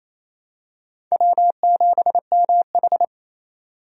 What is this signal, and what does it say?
Morse code, heard as a keyed tone, about 28 wpm, 700 Hz. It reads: W7MH
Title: USA Call Signs - 4 Characters - 28wpm